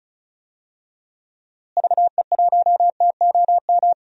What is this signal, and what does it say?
Morse code, heard as a keyed tone, about 34 wpm, 700 Hz. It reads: VE1TOM
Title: DXCC-100 Call Signs - 6 Characters - Repeated 3x - 35wpm